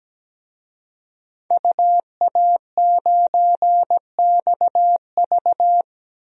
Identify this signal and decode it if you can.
Morse code, heard as a keyed tone, about 17 wpm, 700 Hz. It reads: UA9XV
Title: DXCC-100 Call Signs - 5 Characters - Repeated 3x - 17wpm